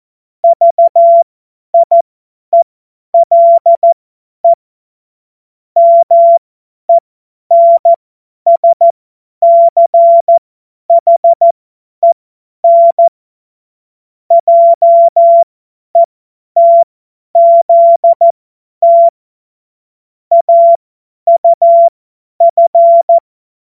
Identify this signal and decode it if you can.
Morse code, heard as a keyed tone, about 14 wpm, spaced as for 10 wpm, 680 Hz. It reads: VIELE MENSCHEN JETZT AUF
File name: Morsekurs / Witze / Witz 22. Juni 2021, 50 F70